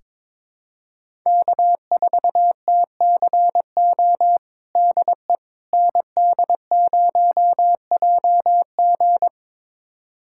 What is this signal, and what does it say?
Morse code, heard as a keyed tone, about 23 wpm, 700 Hz. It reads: K4TCO DE ND0JG
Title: USA Callsign DE Callsign - 5 Characters - 22wpm